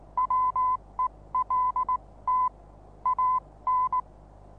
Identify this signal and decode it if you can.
Morse code, heard as a keyed tone, about 19 wpm, spaced as for 16 wpm, 1010 Hz. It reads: WELT AN
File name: repos / Morsekurs / DJ4UF Morsen26